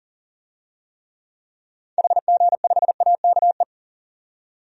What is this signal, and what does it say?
Morse code, heard as a keyed tone, about 40 wpm, 700 Hz. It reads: HG5AKE
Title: DXCC-100 Call Signs - 6 Characters - 40wpm